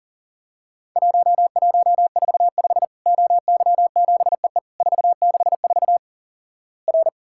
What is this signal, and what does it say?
Morse code, heard as a keyed tone, about 38 wpm, 700 Hz. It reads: 1145 OY7EE 464 R
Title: Worked All Europe DX Contest - QTC Exchange - 40wpm